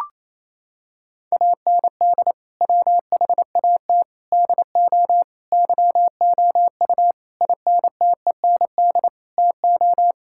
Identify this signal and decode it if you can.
Morse code, heard as a keyed tone, about 28 wpm, 700 Hz. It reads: AND WHAT DO YOU INTEND TO
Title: Sentences from Top 1600 Words 28wpm